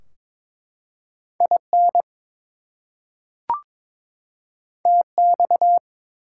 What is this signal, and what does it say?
Morse code, heard as a keyed tone, about 22 wpm, 700 Hz. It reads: IN TX
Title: US States and Canadian Provinces 22wpm